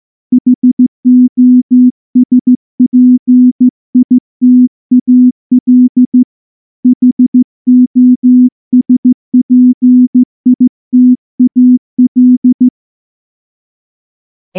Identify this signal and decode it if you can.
Morse code, heard as a keyed tone, about 14 wpm, 250 Hz. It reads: HOSPITAL HOSPITAL
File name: morse / morse-training-15WPM